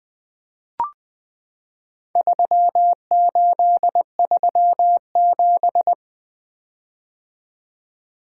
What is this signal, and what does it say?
Morse code, heard as a keyed tone, about 20 wpm, 700 Hz. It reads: 3837
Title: Four Letter-Number Combinations - 20wpm